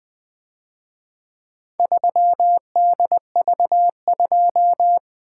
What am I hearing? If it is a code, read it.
Morse code, heard as a keyed tone, about 24 wpm, 700 Hz. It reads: 3DV2
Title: Speed-Racing - Four Letter-Number Combinations - 20wpm